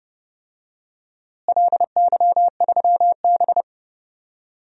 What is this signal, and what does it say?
Morse code, heard as a keyed tone, about 30 wpm, 700 Hz. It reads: LY3B
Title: DXCC-100 Call Signs - 4 Characters - Repeated 3x - 30wpm